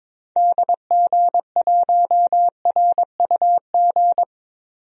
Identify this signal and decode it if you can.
Morse code, heard as a keyed tone, about 22 wpm, 700 Hz. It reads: DG1RUG
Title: DXCC-100 Call Signs - 6 Characters - 22wpm